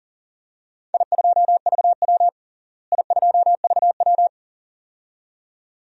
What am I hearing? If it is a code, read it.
Morse code, heard as a keyed tone, about 40 wpm, 700 Hz. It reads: I2VW I2VW
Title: DXCC-100 Call Signs - 4 Characters - Repeated 2x - 40wpm